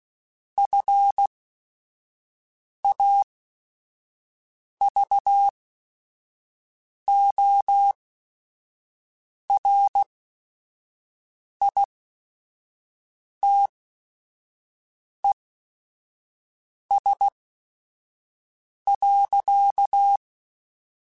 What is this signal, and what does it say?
Morse code, heard as a keyed tone, about 16 wpm, spaced as for 5 wpm, 800 Hz. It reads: FAVORITES.
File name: QSOs / QSOs-mp3s / WD0FBT 1@5